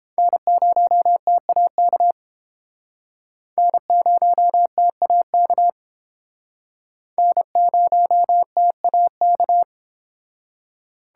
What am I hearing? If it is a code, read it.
Morse code, heard as a keyed tone, about 31 wpm, 700 Hz. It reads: N0TAK N0TAK N0TAK